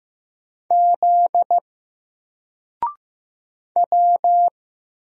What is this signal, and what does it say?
Morse code, heard as a keyed tone, about 15 wpm, 700 Hz. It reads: Z W
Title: Single Letter-Number Alt 15wpm